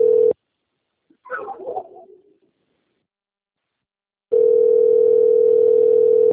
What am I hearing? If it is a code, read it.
Ringback tone (North America), 440 + 480 Hz.